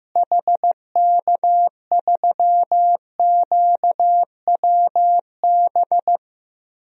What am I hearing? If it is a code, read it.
Morse code, heard as a keyed tone, about 15 wpm, 700 Hz. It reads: HK3QWB
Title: DXCC-100 Call Signs - 6 Characters - Repeated 2x - 15wpm